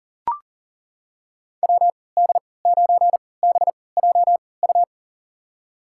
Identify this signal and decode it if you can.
Morse code, heard as a keyed tone, about 39 wpm, spaced as for 22 wpm, 700 Hz. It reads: WD9BJU
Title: USA Call Signs - 6 Characters - 3x Character Spacing - 40wpm